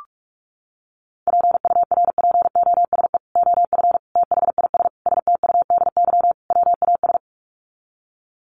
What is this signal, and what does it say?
Morse code, heard as a keyed tone, about 44 wpm, 700 Hz. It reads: PURPOSE OF THIS STUDY WAS
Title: Sets of 5 Words 45wpm